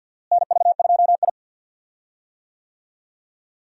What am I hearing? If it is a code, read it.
Morse code, heard as a keyed tone, about 51 wpm, 700 Hz. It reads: NV2I
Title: USA Call Signs - 4 Characters - Repeated 2x - 50wpm